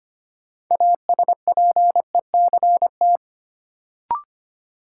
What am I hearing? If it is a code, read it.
Morse code, heard as a keyed tone, about 25 wpm, 700 Hz. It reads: ASPECT
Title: Sentences from Top 1300 Words - New Word Drill Alt - 25wpm